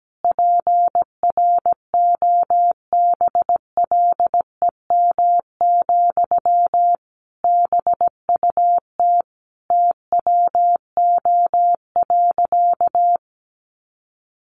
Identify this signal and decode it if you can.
Morse code, heard as a keyed tone, about 17 wpm, 700 Hz. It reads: PROBLEM, BUT TWO.
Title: Sentences from Top 1400 Words 17wpm